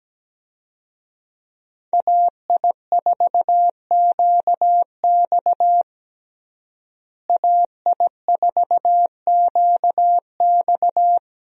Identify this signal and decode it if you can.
Morse code, heard as a keyed tone, about 17 wpm, 700 Hz. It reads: AI4QX AI4QX
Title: USA Call Signs - 5 Characters - Repeated 2x - 17wpm